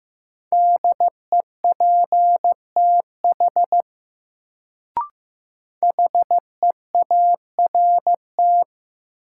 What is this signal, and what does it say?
Morse code, heard as a keyed tone, about 15 wpm, 700 Hz. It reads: DEPTH HEART
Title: Sentences from Top 1200 Words - New Word Drill - 15wpm